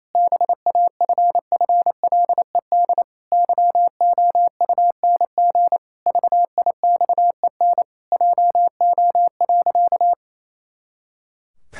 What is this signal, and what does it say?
Morse code, heard as a keyed tone, about 28 wpm, 700 Hz. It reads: BAFFLED YOUNG VIXEN JO.
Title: Pangram Challenge v2 - 28wpm